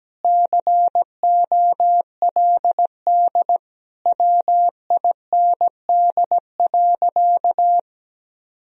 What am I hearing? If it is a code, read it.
Morse code, heard as a keyed tone, about 17 wpm, 700 Hz. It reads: COLD WIND.
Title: Sentences from Top 1000 Words - Review Part 2 - 17wpm